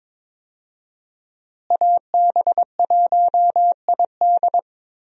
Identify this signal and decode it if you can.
Morse code, heard as a keyed tone, about 22 wpm, 700 Hz. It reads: AB1ID
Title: USA Call Signs - 5 Characters - Repeated 3x - 22wpm